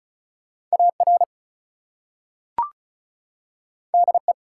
Morse code, heard as a keyed tone, about 36 wpm, 700 Hz. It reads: AR DE
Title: US States and Canadian Provinces 35wpm